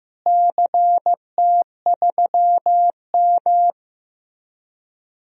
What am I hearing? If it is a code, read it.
Morse code, heard as a keyed tone, about 15 wpm, 700 Hz. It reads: CT3M
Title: DXCC-100 Call Signs - 4 Characters - Repeated 2x - 15wpm